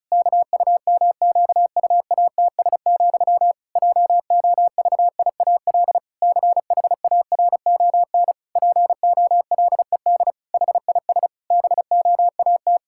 Morse code, heard as a keyed tone, about 34 wpm, 700 Hz. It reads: KUMQUATS, JOVIAL CHARON POLED HIS BOAT
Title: Pangram Challenge v2 - 35wpm